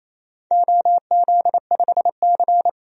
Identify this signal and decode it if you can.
Morse code, heard as a keyed tone, about 28 wpm, 700 Hz. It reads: OZ5C
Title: DXCC-100 Call Signs - 4 Characters - 28wpm